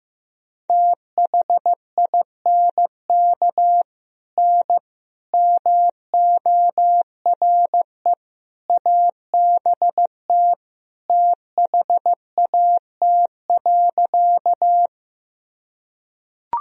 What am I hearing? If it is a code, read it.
Morse code, heard as a keyed tone, about 16 wpm, 700 Hz. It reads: THINK N MORE ABT THAT.
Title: Sentences from Top 200 Words and Abbreviations 15wpm